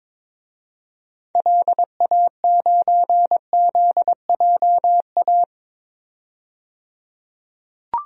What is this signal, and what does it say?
Morse code, heard as a keyed tone, about 22 wpm, 700 Hz. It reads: LA9ZJA
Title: DXCC-100 Call Signs - 6 Characters - 22wpm